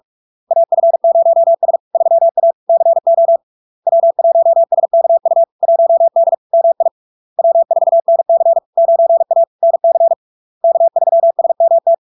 Morse code, heard as a keyed tone, about 46 wpm, 670 Hz. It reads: AF0S 3A CO W1SKU 1D MI W4NC 9A NC K3SMT
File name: LARK-FD-2022-Log-in-CW-45wpm-670-Hz